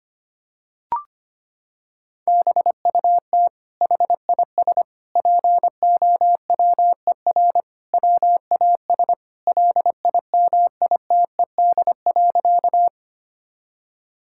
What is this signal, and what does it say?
Morse code, heard as a keyed tone, about 25 wpm, 700 Hz. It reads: BUT HIS POWER WAS LIMITED.
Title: Sentences from Top 1000 Words - Review Part 6 - 25wpm